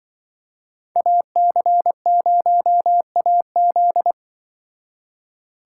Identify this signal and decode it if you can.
Morse code, heard as a keyed tone, about 24 wpm, 700 Hz. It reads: AC0AZ
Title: Speed Racing - USA 2x2 Call Signs 20wpm